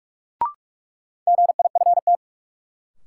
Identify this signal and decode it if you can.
Morse code, heard as a keyed tone, about 43 wpm, 700 Hz. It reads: GIFT
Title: Sentences from Top 900 Words - New Word Drill - 45wpm